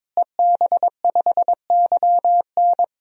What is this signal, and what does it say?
Morse code, heard as a keyed tone, about 22 wpm, 700 Hz. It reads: EB5YN
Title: DXCC-100 Call Signs - 5 Characters - Repeated 2x - 22wpm